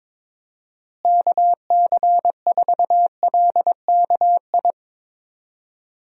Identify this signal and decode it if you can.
Morse code, heard as a keyed tone, about 22 wpm, 700 Hz. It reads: KC4LKI
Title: USA Call Signs - 6 Characters - Repeated 3x - 22wpm